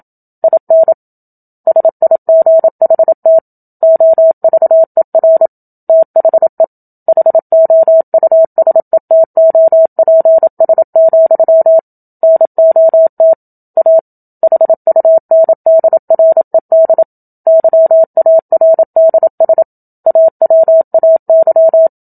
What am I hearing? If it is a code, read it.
Morse code, heard as a keyed tone, about 27 wpm, 650 Hz. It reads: IN SIGHT OVER THE HOUSETOPS, NOT A HUNDRED YARDS AWAY
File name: Book Two / 01-Chapter 1 - 27 WPM